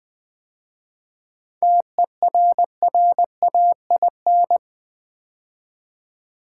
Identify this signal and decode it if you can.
Morse code, heard as a keyed tone, about 20 wpm, 700 Hz. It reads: TERRAIN
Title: 7 Letter Words - First 3 Letters Given - 20wpm